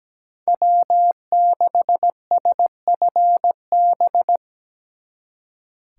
Morse code, heard as a keyed tone, about 20 wpm, 700 Hz. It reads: W6SFB
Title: Speed Racing - USA 1x3 Call Signs 17wpm